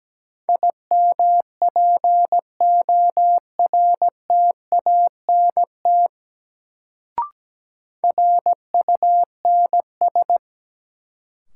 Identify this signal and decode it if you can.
Morse code, heard as a keyed tone, about 17 wpm, 700 Hz. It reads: IMPORTANT RUNS
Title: Sentences from Top 600 Words - New Word Drill - 17wpm